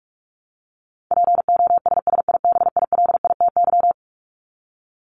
Morse code, heard as a keyed tone, about 44 wpm, 700 Hz. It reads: POSSIBILITY
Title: Top 1751-2000 Words 45wpm